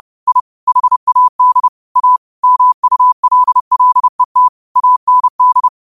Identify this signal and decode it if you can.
Morse code, heard as a keyed tone, about 30 wpm, 1000 Hz. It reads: I HAD A MULLET AND